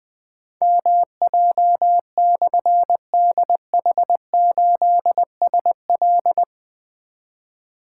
Morse code, heard as a keyed tone, about 20 wpm, 700 Hz. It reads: MJ/DH8SL
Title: Difficult Call Signs 20wpm